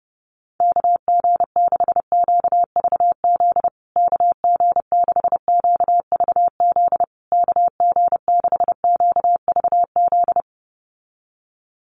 Morse code, heard as a keyed tone, about 30 wpm, 700 Hz. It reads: KG6QVZ KG6QVZ KG6QVZ
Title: USA Call Signs Repeated 3x 30wpm